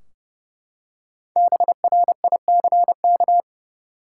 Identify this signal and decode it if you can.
Morse code, heard as a keyed tone, about 30 wpm, 700 Hz. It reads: BRICK
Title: Five Letter Words - 30wpm